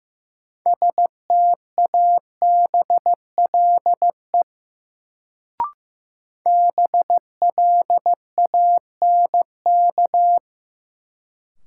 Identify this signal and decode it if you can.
Morse code, heard as a keyed tone, about 15 wpm, 700 Hz. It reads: STABLE BLANK
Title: Sentences from Top 1500 Words - New Word Drill Alt - 15wpm